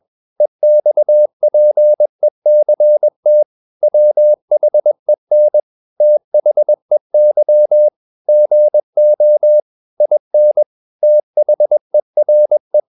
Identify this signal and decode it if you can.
Morse code, heard as a keyed tone, about 21 wpm, 600 Hz. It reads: EXPECT WHEN THEY GO IN THERE
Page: CW med Gnister